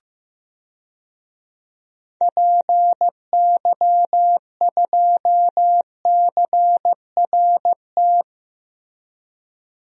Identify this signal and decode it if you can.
Morse code, heard as a keyed tone, about 15 wpm, 700 Hz. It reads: PY2CRT
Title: DXCC-100 Call Signs - 6 Characters - 15wpm